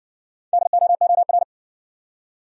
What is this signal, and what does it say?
Morse code, heard as a keyed tone, about 60 wpm, 700 Hz. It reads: DOOR